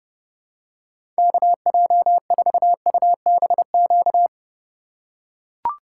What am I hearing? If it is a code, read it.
Morse code, heard as a keyed tone, about 30 wpm, 700 Hz. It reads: KJ4UBQ
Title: USA Call Signs - 6 Characters - 3x Character Spacing - 30wpm